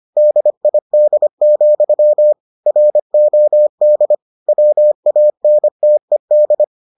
Morse code, heard as a keyed tone, about 25 wpm, 600 Hz. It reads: DID, ROD WANTED